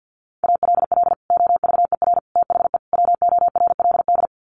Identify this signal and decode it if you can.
Morse code, heard as a keyed tone, about 50 wpm, 700 Hz. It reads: ALL OVER THE WORLD
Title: Sets of 4 Words 50wpm